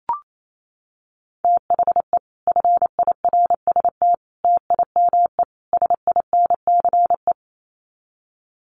Morse code, heard as a keyed tone, about 28 wpm, 700 Hz. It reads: THE FIRST TIME SINCE
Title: Sets of 4 Words 28wpm